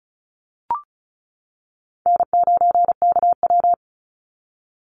Morse code, heard as a keyed tone, about 34 wpm, 700 Hz. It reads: N9KW
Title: NA SOTA Chaser Callsigns 35wpm